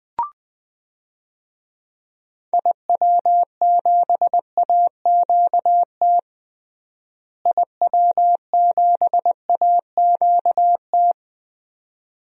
Morse code, heard as a keyed tone, about 21 wpm, 700 Hz. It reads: IW7AQT IW7AQT
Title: DXCC-100 Call Signs - 6 Characters - Repeated 2x - 20wpm